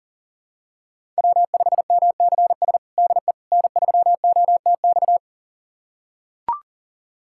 Morse code, heard as a keyed tone, about 40 wpm, 700 Hz. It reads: W5MCS DE N3OTX
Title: USA Callsign DE Callsign - 5 Characters - 40wpm